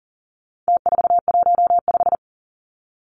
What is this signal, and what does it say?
Morse code, heard as a keyed tone, about 40 wpm, 700 Hz. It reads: T415
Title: Numbers - Contest Style - 4-Digit 40wpm